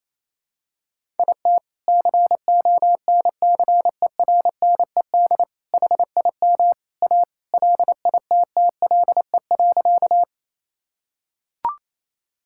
Morse code, heard as a keyed tone, about 28 wpm, 700 Hz. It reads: IT CONCERNED HIM A LITTLE.
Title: Sentences from Top 1000 Words - Review Part 6 - 28wpm